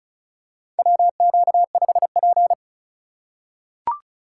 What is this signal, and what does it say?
Morse code, heard as a keyed tone, about 34 wpm, 700 Hz. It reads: WQ5P
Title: USA Call Signs - 4 Characters - 2x Character Spacing - 35wpm